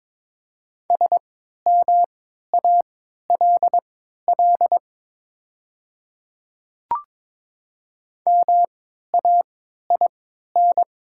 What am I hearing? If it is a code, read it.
Morse code, heard as a keyed tone, about 22 wpm, spaced as for 12 wpm, 700 Hz. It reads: SMALL MAIN
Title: 3x Character Spacing - Top 500 Words - 22wpm